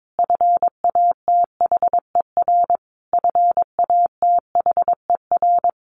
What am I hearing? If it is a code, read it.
Morse code, heard as a keyed tone, about 22 wpm, 700 Hz. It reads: FATHER FATHER
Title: Top 500 Words Repeated 22wpm